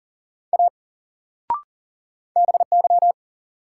Morse code, heard as a keyed tone, about 40 wpm, 700 Hz. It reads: A BY
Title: Sentences from Top 100 Words - New Word Drill - 40wpm